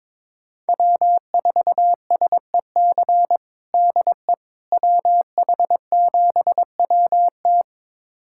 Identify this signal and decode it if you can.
Morse code, heard as a keyed tone, about 22 wpm, 700 Hz. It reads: W4SEC DE WH7WT
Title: USA Callsign DE Callsign - 5 Characters - 22wpm